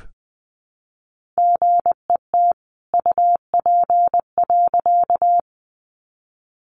Morse code, heard as a keyed tone, about 20 wpm, 700 Hz. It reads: GET UP.
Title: A Sound Of Thunder 20wpm